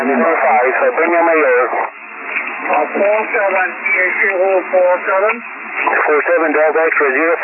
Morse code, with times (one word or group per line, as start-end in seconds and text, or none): none